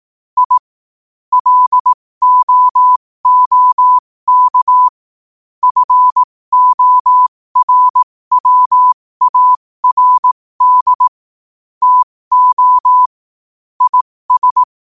Morse code, heard as a keyed tone, about 18 wpm, spaced as for 15 wpm, 1000 Hz. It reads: I LOOK FORWARD TO IS
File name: qotdfiles / 15wpm